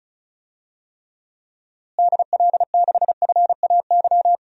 Morse code, heard as a keyed tone, about 34 wpm, 700 Hz. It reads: DL6FAY
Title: DXCC-100 Call Signs - 6 Characters - Repeated 3x - 35wpm